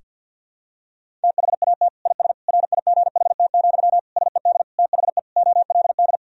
Morse code, heard as a keyed tone, about 50 wpm, 700 Hz. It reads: THAT IS RIGHT, SED THE OLD
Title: Sentences from Top 200 Words and Abbreviations 50wpm